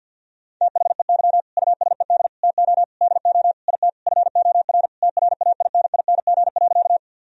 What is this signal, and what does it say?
Morse code, heard as a keyed tone, about 50 wpm, 700 Hz. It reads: THEY USED TO DO IT FOR TRAINING.
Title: Sentences from Top 1000 Words - Review Part 4 - 50wpm